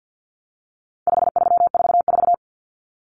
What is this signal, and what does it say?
Morse code, heard as a keyed tone, about 48 wpm, 700 Hz. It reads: 5344
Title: Numbers - 4-Digit 50wpm